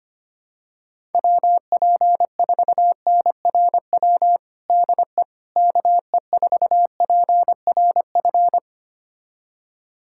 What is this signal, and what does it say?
Morse code, heard as a keyed tone, about 25 wpm, 700 Hz. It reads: WP4NRW DE KE4PRF
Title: USA Callsign DE Callsign - 6 Characters - 25wpm